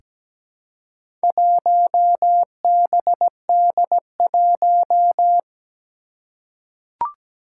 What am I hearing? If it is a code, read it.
Morse code, heard as a keyed tone, about 17 wpm, 700 Hz. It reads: 1BD1
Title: Four Challenging Letter-Number Combinations 17wpm